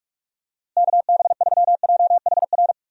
Morse code, heard as a keyed tone, about 46 wpm, 700 Hz. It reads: KB3JHR
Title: USA Call Signs - 6 Characters - Repeated 2x - 45wpm